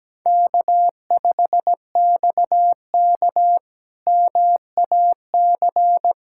Morse code, heard as a keyed tone, about 17 wpm, 700 Hz. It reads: K5XK MAC
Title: K1USN SST QSO Practice - 17wpm